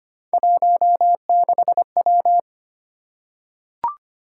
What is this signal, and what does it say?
Morse code, heard as a keyed tone, about 25 wpm, 700 Hz. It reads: 16W
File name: Three Letter-Number Combinations - 25wpm